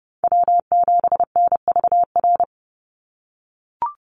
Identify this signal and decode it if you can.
Morse code, heard as a keyed tone, about 30 wpm, 700 Hz. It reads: W7NVR
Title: USA Call Signs 30wpm